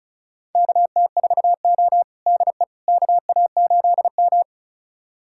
Morse code, heard as a keyed tone, about 34 wpm, 700 Hz. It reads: KT4O DE KA8M
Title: USA Callsign DE Callsign - 4 Characters - 35wpm